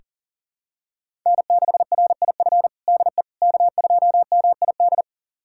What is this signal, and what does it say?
Morse code, heard as a keyed tone, about 40 wpm, 700 Hz. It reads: N6RIF DE K2MID
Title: USA Callsign DE Callsign - 5 Characters - 40wpm